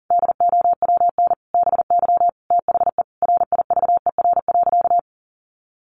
Morse code, heard as a keyed tone, about 40 wpm, 700 Hz. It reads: DOWN BY THE RIVER.
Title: Sentences from Top 1200 Words 40wpm